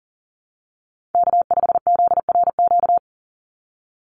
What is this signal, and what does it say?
Morse code, heard as a keyed tone, about 40 wpm, 700 Hz. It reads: K5ZRQ
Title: USA Call Signs 40wpm